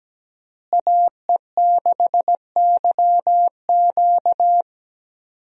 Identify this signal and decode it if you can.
Morse code, heard as a keyed tone, about 17 wpm, 700 Hz. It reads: AE6YQ
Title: USA Call Signs - 5 Characters - Repeated 3x - 17wpm